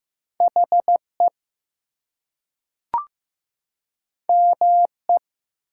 Morse code, heard as a keyed tone, about 15 wpm, 700 Hz. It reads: HE ME
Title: Two Letter Words - 15wpm